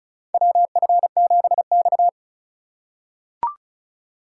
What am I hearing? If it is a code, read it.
Morse code, heard as a keyed tone, about 36 wpm, 700 Hz. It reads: WF7X
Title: USA Call Signs - 3x Character Spacing - 35wpm